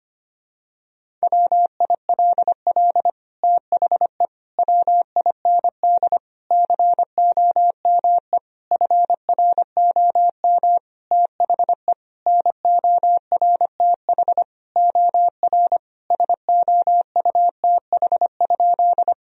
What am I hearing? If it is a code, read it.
Morse code, heard as a keyed tone, about 25 wpm, 700 Hz. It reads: WILL THE WIND COME FROM THE NORTH OR SOUTH?
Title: Sentences from Top 1000 Words - Review Part 2 - 25wpm